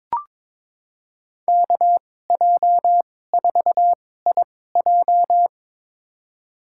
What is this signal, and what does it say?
Morse code, heard as a keyed tone, about 22 wpm, spaced as for 16 wpm, 700 Hz. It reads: KJ4IJ
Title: USA Call Signs - 5 Characters - 2x Character Spacing - 22wpm